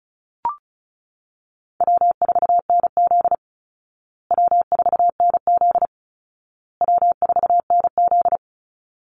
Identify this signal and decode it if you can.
Morse code, heard as a keyed tone, about 34 wpm, 700 Hz. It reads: W4NZ W4NZ W4NZ
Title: Easy Call Signs Repeated 3x - 35wpm